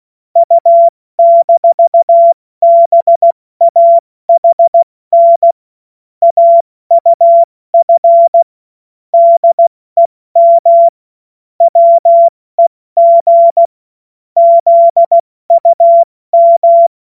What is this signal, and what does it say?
Morse code, heard as a keyed tone, about 16 wpm, 680 Hz. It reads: U-BAHN AUF DEM WEG ZUM
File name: Morsekurs / Witze / Witz 13. Dezember 2022 70 F80